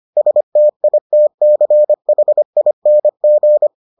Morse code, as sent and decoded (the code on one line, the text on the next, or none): ... - .. - -.-. .... .. -. --.
STITCHING